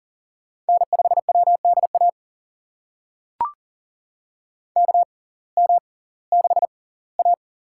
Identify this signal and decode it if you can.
Morse code, heard as a keyed tone, about 40 wpm, spaced as for 22 wpm, 700 Hz. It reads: N5WDA KM6A
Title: USA Call Signs - 6x Character Spacing - 40wpm